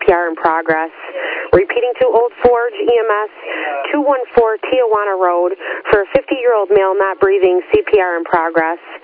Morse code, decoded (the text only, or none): none